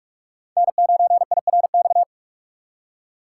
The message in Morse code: N9IRX

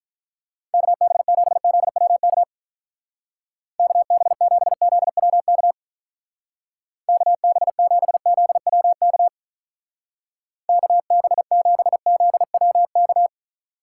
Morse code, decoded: KB7ZWK KB7ZWK KB7ZWK KB7ZWK

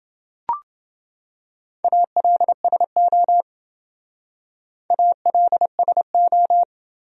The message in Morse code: ALSO ALSO

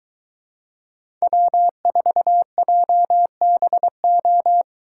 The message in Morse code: W4JBO

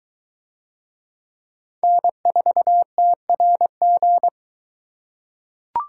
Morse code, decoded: N4TRG

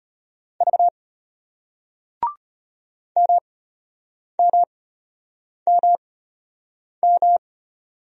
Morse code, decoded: V MMMM

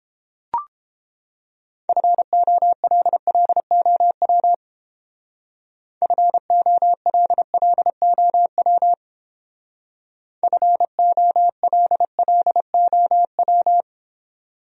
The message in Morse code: FOLLOW FOLLOW FOLLOW

